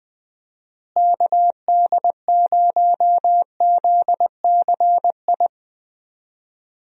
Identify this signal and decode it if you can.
Morse code, heard as a keyed tone, about 20 wpm, 700 Hz. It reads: KD0ZCI
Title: USA Call Signs - 6 Characters - Repeated 3x - 20wpm